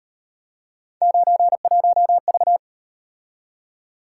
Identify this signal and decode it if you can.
Morse code, heard as a keyed tone, about 38 wpm, 700 Hz. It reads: 91V